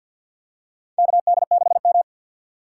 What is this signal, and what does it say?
Morse code, heard as a keyed tone, about 46 wpm, 700 Hz. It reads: KD6M
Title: USA Call Signs - 4 Characters - 2x Character Spacing - 50wpm